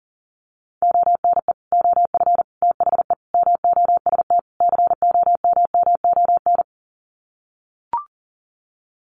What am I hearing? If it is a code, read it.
Morse code, heard as a keyed tone, about 40 wpm, 700 Hz. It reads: ONE OF THE MOST COMMON
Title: Sets of 5 Words Encore2 40wpm